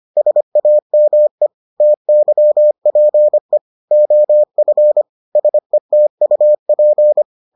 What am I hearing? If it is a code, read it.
Morse code, heard as a keyed tone, about 25 wpm, 600 Hz. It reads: SAME TYPE OF SETUP